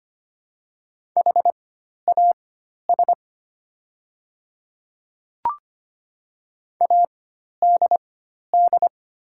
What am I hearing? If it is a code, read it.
Morse code, heard as a keyed tone, about 25 wpm, spaced as for 12 wpm, 700 Hz. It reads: HAS ADD
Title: 4x Character Spacing - Top 200 Words - 25wpm